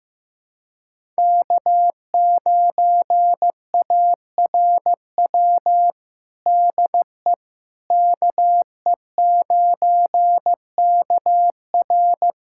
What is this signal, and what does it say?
Morse code, heard as a keyed tone, about 15 wpm, 700 Hz. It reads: K9ARW DE KE9KR
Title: USA Callsign DE Callsign - 5 Characters - 15wpm